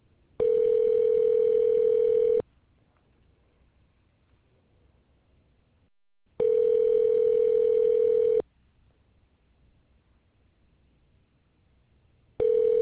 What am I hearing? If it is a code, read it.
Ringback tone (North America), 440 + 480 Hz.